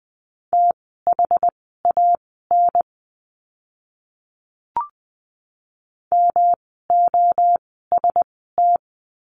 Morse code, heard as a keyed tone, about 20 wpm, spaced as for 14 wpm, 700 Hz. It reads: THAN MOST
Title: 115 Lesson 19f - Letter w - Words 20wpm